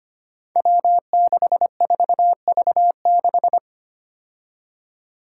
Morse code, heard as a keyed tone, about 25 wpm, 700 Hz. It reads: W64V6
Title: Five Challenging Letter-Number Combinations 25wpm